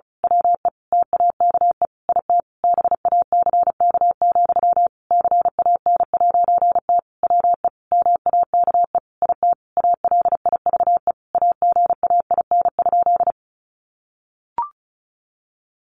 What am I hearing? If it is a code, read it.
Morse code, heard as a keyed tone, about 34 wpm, 700 Hz. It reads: WE TAKE IT BACK, CAN'T WE MAKE IT ALIVE AGAIN?
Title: A Sound Of Thunder 35wpm